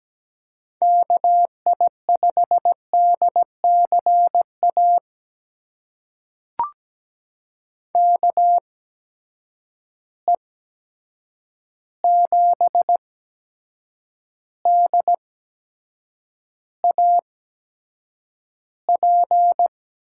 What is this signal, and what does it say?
Morse code, heard as a keyed tone, about 17 wpm, spaced as for 7 wpm, 700 Hz. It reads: KI5DCA KE7DAP
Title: USA Call Signs - 6 Characters - 8x Character Spacing - 17wpm